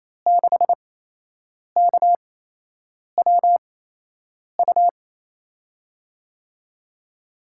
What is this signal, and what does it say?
Morse code, heard as a keyed tone, about 28 wpm, spaced as for 8 wpm, 700 Hz. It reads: 6KWU
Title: USA Call Signs - 5 Characters - 8x Character Spacing - 28wpm